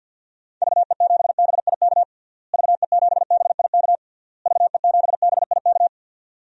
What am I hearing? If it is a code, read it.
Morse code, heard as a keyed tone, about 53 wpm, 700 Hz. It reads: VE7BIK VE7BIK VE7BIK
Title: DXCC-100 Call Signs - 6 Characters - Repeated 3x - 50wpm